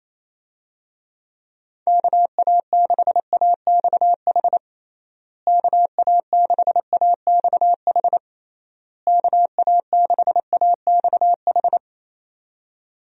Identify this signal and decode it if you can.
Morse code, heard as a keyed tone, about 28 wpm, 700 Hz. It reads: KA6AXH KA6AXH KA6AXH
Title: USA Call Signs - 6 Characters - Repeated 3x - 28wpm